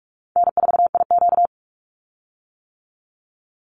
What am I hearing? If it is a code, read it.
Morse code, heard as a keyed tone, about 44 wpm, 700 Hz. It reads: N4IQ
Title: Call Signs - 45wpm